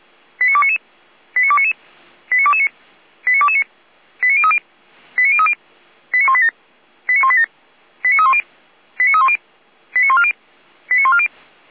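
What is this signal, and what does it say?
Five-tone selective call (ZVEI), heard as quick runs of five tones, tones of about 70 ms. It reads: 89299 89299 89229 89229 89030 89030 89187 89187 89310 89310 89140 89140
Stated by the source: ZVEI five-tone selective call 89240 received on SDR, on the Feuerwehr Orlamünde (f85360) audio server.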